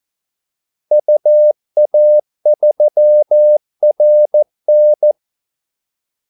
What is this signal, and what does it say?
Morse code, heard as a keyed tone, about 14 wpm, 600 Hz. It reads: UA3RN
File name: Callsigns-14wpm